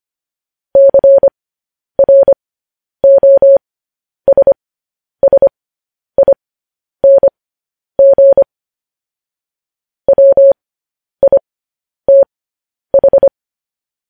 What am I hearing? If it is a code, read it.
Morse code, heard as a keyed tone, about 25 wpm, spaced as for 10 wpm, 550 Hz. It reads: CROSSING WITH
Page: news headlines in morse code at 10 wpm